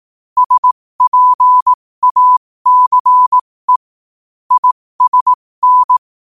Morse code, heard as a keyed tone, about 18 wpm, spaced as for 15 wpm, 1000 Hz. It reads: SPACE ISN
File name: qotdfiles / 15wpm